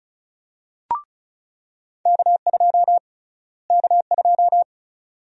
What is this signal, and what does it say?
Morse code, heard as a keyed tone, about 36 wpm, 700 Hz. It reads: K2 K2
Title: USA Call Sign Prefixes - Repeated 2x - 35wpm